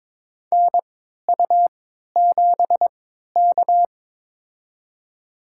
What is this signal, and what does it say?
Morse code, heard as a keyed tone, about 22 wpm, spaced as for 13 wpm, 690 Hz. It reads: NU7K
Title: USA Call Signs - 4 Characters - 3x Character Spacing - 22wpm